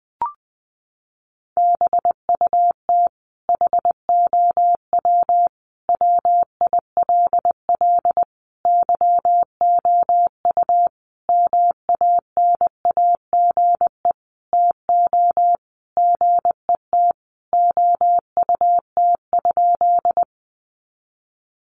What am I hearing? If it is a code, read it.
Morse code, heard as a keyed tone, about 20 wpm, 700 Hz. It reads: BUT HOW WILL YOU MANAGE TO GET OUT?
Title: Sentences from Top 1400 Words 20wpm